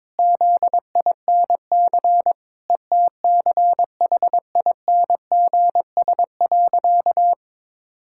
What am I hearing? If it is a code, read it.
Morse code, heard as a keyed tone, about 22 wpm, 700 Hz. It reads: ZINC ETCHINGS.
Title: Pangram Challenge v2 - 22wpm